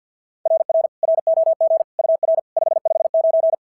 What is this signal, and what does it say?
Morse code, heard as a keyed tone, about 46 wpm, 690 Hz. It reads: RR ROG UR 559
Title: SOTA QSO Practice 50wpm